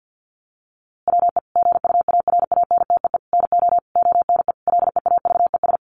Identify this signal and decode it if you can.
Morse code, heard as a keyed tone, about 50 wpm, 700 Hz. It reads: WE GUARANTEE NO ONE LEAVES
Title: A Sound Of Thunder 50wpm